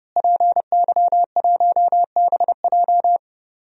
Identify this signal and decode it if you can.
Morse code, heard as a keyed tone, about 30 wpm, 700 Hz. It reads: PY1BJ
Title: DXCC-100 Call Signs - 5 Characters - Repeated 3x - 30wpm